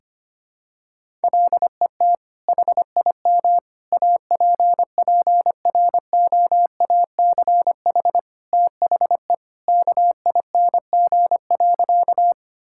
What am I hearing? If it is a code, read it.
Morse code, heard as a keyed tone, about 25 wpm, 700 Hz. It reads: LET HIM APPROACH THE KING.
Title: Sentences from Top 1000 Words - Review Part 6 - 25wpm